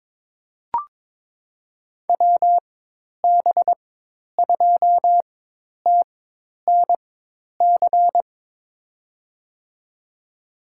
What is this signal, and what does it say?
Morse code, heard as a keyed tone, about 22 wpm, spaced as for 10 wpm, 700 Hz. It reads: WB2TNC
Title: USA Call Signs - 6 Characters - 4x Character Spacing - 22wpm